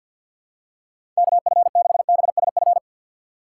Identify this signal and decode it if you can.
Morse code, heard as a keyed tone, about 51 wpm, 700 Hz. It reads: KF6BSF
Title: USA Call Signs - 6 Characters - 4x Character Spacing - 50wpm